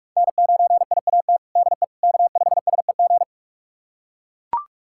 Morse code, heard as a keyed tone, about 41 wpm, 700 Hz. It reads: N9IAT DE K5SEG
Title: USA Callsign DE Callsign - 5 Characters - 45wpm